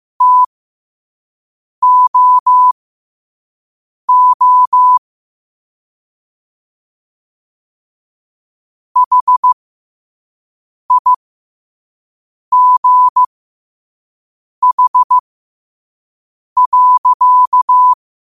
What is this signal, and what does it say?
Morse code, heard as a keyed tone, about 15 wpm, spaced as for 5 wpm, 1000 Hz. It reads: TOO HIGH.
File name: qotdfiles / 5wpm